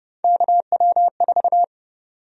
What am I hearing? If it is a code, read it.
Morse code, heard as a keyed tone, about 30 wpm, 700 Hz. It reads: KW4
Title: USA Call Sign Prefixes - Repeated 2x - 30wpm